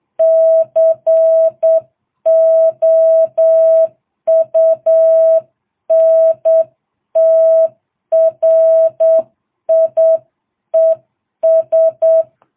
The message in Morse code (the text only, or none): COUNTRIES